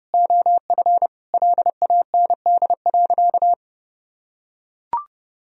Morse code, heard as a keyed tone, about 32 wpm, 700 Hz. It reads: OF LAND.
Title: Sentences from Top 1000 Words - Review Part 2 - 30wpm